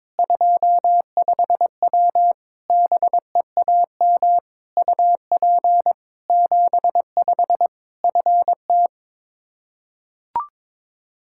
Morse code, heard as a keyed tone, about 22 wpm, 700 Hz. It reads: 25W BEAM UP 75 FT
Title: Practice QSO - Exchange 2 - Info - 22wpm